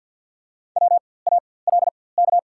Morse code, heard as a keyed tone, about 53 wpm, spaced as for 24 wpm, 690 Hz. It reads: WALK
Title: 4x Character Spacing - Top 500 Words - 50wpm